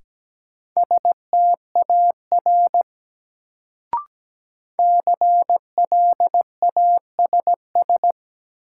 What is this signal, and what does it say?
Morse code, heard as a keyed tone, about 17 wpm, 700 Hz. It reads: STAR CLASS
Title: Sentences from Top 1000 Words - Review Part 2 - New Word Drill - 17wpm